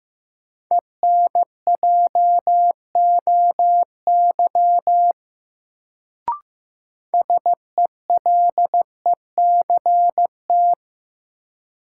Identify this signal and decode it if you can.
Morse code, heard as a keyed tone, about 15 wpm, 700 Hz. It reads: ENJOY SELECT
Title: Sentences from Top 1300 Words - New Word Drill Alt - 15wpm